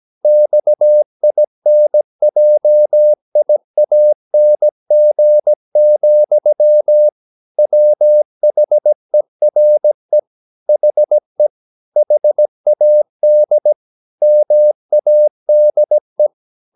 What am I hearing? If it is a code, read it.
Morse code, heard as a keyed tone, about 17 wpm, 600 Hz. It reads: XINJIANG, WHERE HE HAD MADE
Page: Romanen Never af Ken Follet på engelsk.